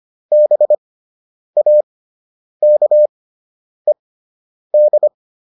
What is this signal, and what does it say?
Morse code, heard as a keyed tone, about 25 wpm, spaced as for 9 wpm, 600 Hz. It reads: BAKED
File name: ED5 09